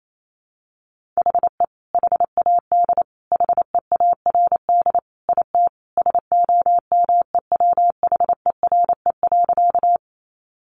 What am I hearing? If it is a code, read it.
Morse code, heard as a keyed tone, about 28 wpm, 700 Hz. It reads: HE HAD HEARD IT SOMEWHERE.
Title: Sentences from Top 1400 Words 28wpm